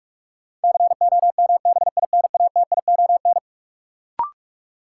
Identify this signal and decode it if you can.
Morse code, heard as a keyed tone, about 44 wpm, 700 Hz. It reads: COMBINATION
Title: Sentences from Top 1300 Words - New Word Drill Alt - 45wpm